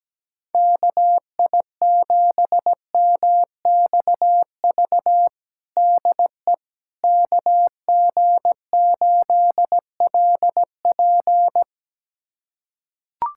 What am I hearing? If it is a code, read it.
Morse code, heard as a keyed tone, about 17 wpm, 700 Hz. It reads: KI7MXV DE KG8LP
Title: USA Callsign DE Callsign 17wpm